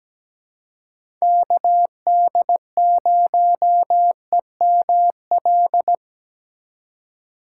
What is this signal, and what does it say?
Morse code, heard as a keyed tone, about 17 wpm, 700 Hz. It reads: KD0EML
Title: USA Call Signs - 4x Character Spacing - 17wpm